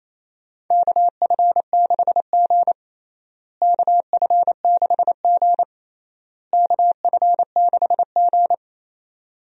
Morse code, heard as a keyed tone, about 28 wpm, 700 Hz. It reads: KF6G KF6G KF6G
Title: USA Call Signs - 4 Characters - Repeated 3x - 28wpm